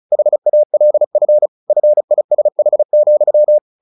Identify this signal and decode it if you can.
Morse code, heard as a keyed tone, about 34 wpm, 600 Hz. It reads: HALF FISH,
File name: the old man and the sea 2 35wpm